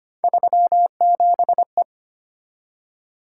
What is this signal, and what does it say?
Morse code, heard as a keyed tone, about 25 wpm, 700 Hz. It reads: 37E
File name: Three Letter-Number Combinations - 25wpm